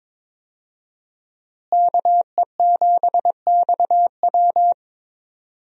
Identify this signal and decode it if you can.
Morse code, heard as a keyed tone, about 23 wpm, 700 Hz. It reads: KE7XW
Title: USA Call Signs - 5 Characters - Repeated 2x - 22wpm